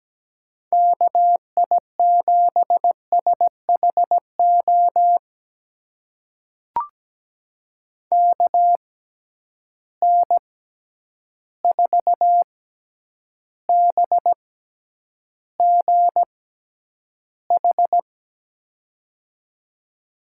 Morse code, heard as a keyed tone, about 17 wpm, spaced as for 9 wpm, 700 Hz. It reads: KI7SHO KN4BGH
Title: USA Call Signs - 6 Characters - 6x Character Spacing - 17wpm